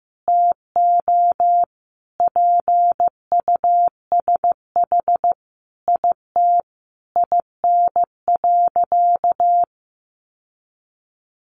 Morse code, heard as a keyed tone, about 15 wpm, 700 Hz. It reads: TO PUSH IT IN.
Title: Sentences from Top 1300 Words 15wpm